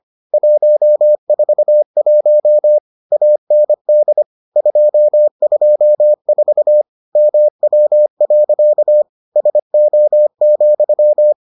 Morse code, heard as a keyed tone, about 25 wpm, 600 Hz. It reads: 141 AND 224 MW. SO,